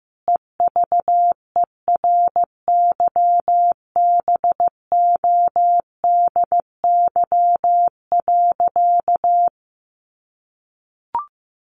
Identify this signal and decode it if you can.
Morse code, heard as a keyed tone, about 14 wpm, 700 Hz. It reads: EVERYBODY.
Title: Sentences from Top 1400 Words 15wpm